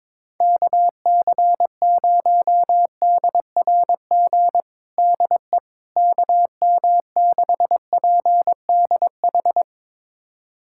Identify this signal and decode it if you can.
Morse code, heard as a keyed tone, about 22 wpm, 700 Hz. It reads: KC0DRG DE KM6PDH
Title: USA Callsign DE Callsign - 6 Characters - 22wpm